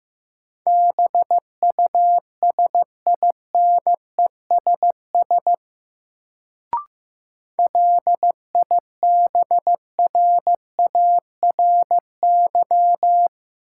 Morse code, heard as a keyed tone, about 15 wpm, 700 Hz. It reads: BUSINESS LIBRARY
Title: Sentences from Top 600 Words - New Word Drill - 15wpm